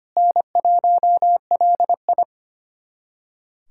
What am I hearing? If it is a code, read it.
Morse code, heard as a keyed tone, about 25 wpm, 700 Hz. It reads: N1LI
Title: USA Callsign DE Callsign - 4 Characters - 25wpm